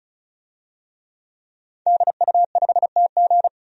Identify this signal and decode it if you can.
Morse code, heard as a keyed tone, about 36 wpm, 700 Hz. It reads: DU5TG
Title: DXCC-100 Call Signs - 5 Characters - 35wpm